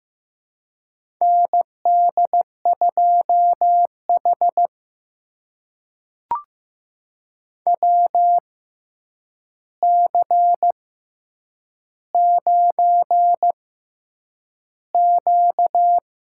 Morse code, heard as a keyed tone, about 15 wpm, spaced as for 8 wpm, 700 Hz. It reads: ND2H WC9Q
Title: USA Call Signs - 4 Characters - 6x Character Spacing - 15wpm